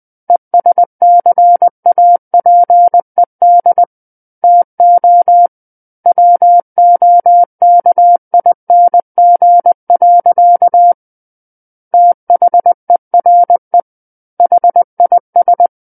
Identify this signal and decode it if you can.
Morse code, heard as a keyed tone, about 20 wpm, 700 Hz. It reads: ESCAPED TO WOKING. THERE HIS
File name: 20 WPM